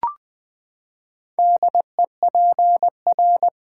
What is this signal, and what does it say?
Morse code, heard as a keyed tone, about 20 wpm, 700 Hz. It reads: DEPR
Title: Build A Word - 4 Letters At A Time - 10-Letter Words - 20wpm